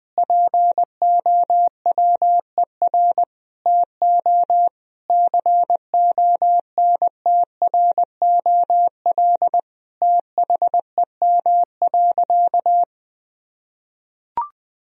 Morse code, heard as a keyed tone, about 20 wpm, 700 Hz. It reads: POWER TO CONTROL THEM.
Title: Sentences from Top 1000 Words - Review Part 2 - 20wpm